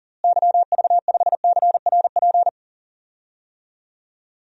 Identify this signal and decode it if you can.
Morse code, heard as a keyed tone, about 40 wpm, 700 Hz. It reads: YV5CRP
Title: DXCC-100 Call Signs - 6 Characters - Repeated 3x - 40wpm